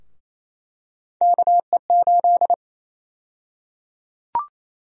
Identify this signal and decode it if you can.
Morse code, heard as a keyed tone, about 27 wpm, 700 Hz. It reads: KE8
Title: USA Call Sign Prefixes - Repeated 2x - 28wpm